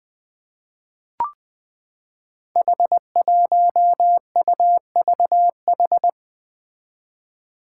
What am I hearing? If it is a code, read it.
Morse code, heard as a keyed tone, about 21 wpm, 700 Hz. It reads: H1UVH
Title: Five Challenging Letter-Number Combinations 20wpm